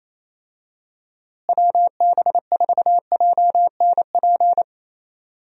Morse code, heard as a keyed tone, about 28 wpm, 700 Hz. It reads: WB4JNP
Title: USA Call Signs - 6 Characters - Repeated 3x - 28wpm